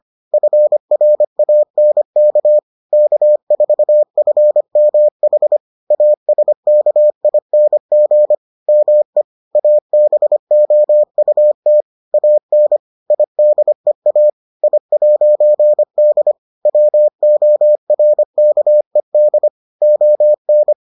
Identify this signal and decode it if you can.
Morse code, heard as a keyed tone, about 25 wpm, 600 Hz. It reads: FRANK K4FMH ASKING ME ABOUT AN IDEA I'D WORKED ON